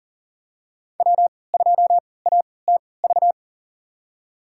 Morse code, heard as a keyed tone, about 40 wpm, spaced as for 23 wpm, 700 Hz. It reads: W2ATV
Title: USA Call Signs - 5 Characters - 3x Character Spacing - 40wpm